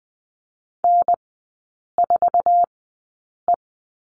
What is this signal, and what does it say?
Morse code, heard as a keyed tone, about 20 wpm, spaced as for 8 wpm, 700 Hz. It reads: N4E
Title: 074 Lesson 13e - Letter L - Three Characters 20wpm